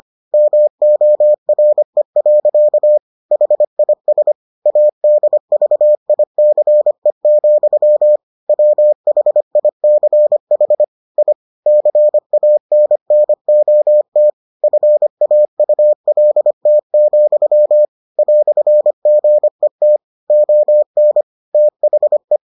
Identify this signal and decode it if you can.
Morse code, heard as a keyed tone, about 25 wpm, 600 Hz. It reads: MORE. HIS ADVICE, WHICH I CANNOT FAULT, "GET ON THE